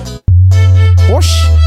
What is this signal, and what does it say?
Speech. There is no code.